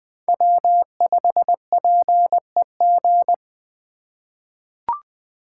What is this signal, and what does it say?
Morse code, heard as a keyed tone, about 20 wpm, 700 Hz. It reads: W5PEG
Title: USA Call Signs - 3x Character Spacing - 20wpm